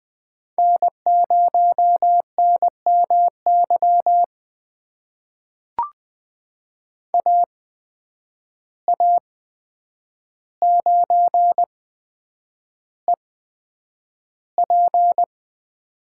Morse code, heard as a keyed tone, about 20 wpm, spaced as for 9 wpm, 700 Hz. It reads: N0NMY AA9EP
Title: USA Call Signs - 5 Characters - 8x Character Spacing - 20wpm